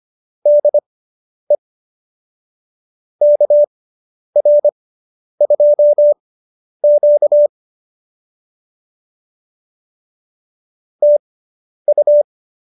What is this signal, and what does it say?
Morse code, heard as a keyed tone, about 25 wpm, spaced as for 10 wpm, 590 Hz. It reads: DE KR2Q TU
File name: QSO220 10